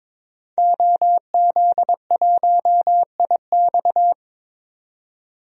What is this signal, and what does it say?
Morse code, heard as a keyed tone, about 22 wpm, 700 Hz. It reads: OZ1IX
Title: DXCC-100 Call Signs - 5 Characters - 22wpm